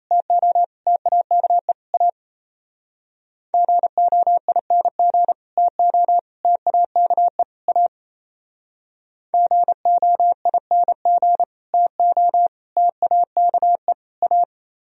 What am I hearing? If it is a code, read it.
Morse code, heard as a keyed tone, about 35 wpm, 700 Hz. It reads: TO TAKE A GOING TO TAKE A GOING TO TAKE A
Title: Speed-Racing - Sets of 4 Words 28wpm